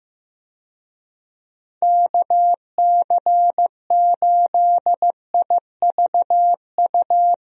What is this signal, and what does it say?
Morse code, heard as a keyed tone, about 16 wpm, 700 Hz. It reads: KC8IVU
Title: USA Call Signs - 6 Characters - Repeated 3x - 15wpm